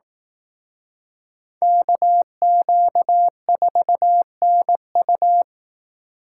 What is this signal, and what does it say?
Morse code, heard as a keyed tone, about 18 wpm, 700 Hz. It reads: KQ4NU
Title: Speed Racing - USA Call Signs 15wpm